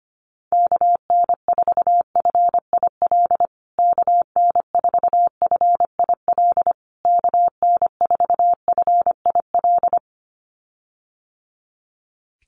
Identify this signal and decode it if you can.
Morse code, heard as a keyed tone, about 25 wpm, 700 Hz. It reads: KN4FIL KN4FIL KN4FIL
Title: USA Call Signs Repeated 3x 25wpm